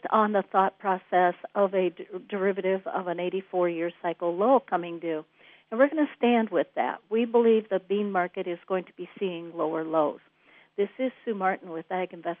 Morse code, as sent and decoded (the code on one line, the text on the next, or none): none